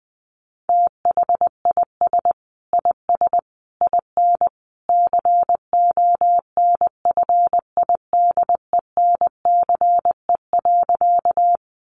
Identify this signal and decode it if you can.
Morse code, heard as a keyed tone, about 20 wpm, 700 Hz. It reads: THIS IS IN CONFIDENCE.
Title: Sentences from Top 1500 Words 20wpm